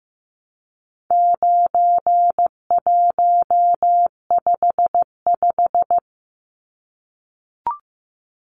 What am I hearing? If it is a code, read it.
Morse code, heard as a keyed tone, about 15 wpm, 700 Hz. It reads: 9155
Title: Numbers - 4-Digit 15wpm